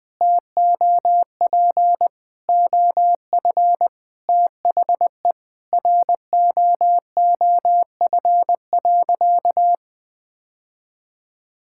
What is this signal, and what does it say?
Morse code, heard as a keyed tone, about 21 wpm, 700 Hz. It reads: TOP OF THE ROOF.
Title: Sentences with 3-Word Phrases - 20wpm